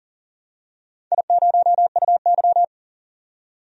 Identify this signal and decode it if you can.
Morse code, heard as a keyed tone, about 40 wpm, 700 Hz. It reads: I0UY
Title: DXCC-100 Call Signs - 4 Characters - 40wpm